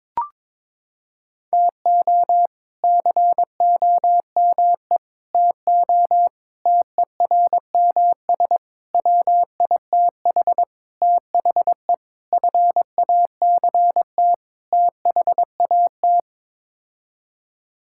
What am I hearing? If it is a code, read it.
Morse code, heard as a keyed tone, about 22 wpm, 700 Hz. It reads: TO COME TO TERMS WITH THE FACT THAT
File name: practice / 22wpm / Sets of 8 Words - 22wpm